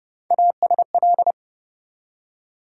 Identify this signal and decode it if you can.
Morse code, heard as a keyed tone, about 30 wpm, 700 Hz. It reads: ASL
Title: USA Call Sign Postfixes - 3 Letters 30wpm